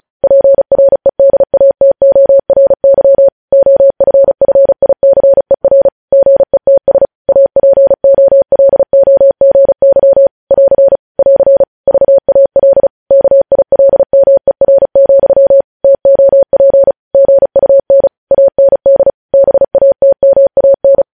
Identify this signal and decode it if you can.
Morse code, heard as a keyed tone, about 36 wpm, 550 Hz. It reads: PREDATORY OFFICER GETS APOLOGY <AR> <AR> VAL KILMER, TOP GUN AND BATMAN
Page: News Headlines in Morse Code at 35 WPM